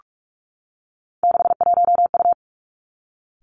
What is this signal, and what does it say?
Morse code, heard as a keyed tone, about 46 wpm, 700 Hz. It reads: 61U
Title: Three Challenging Letter-Number Combinations 45wpm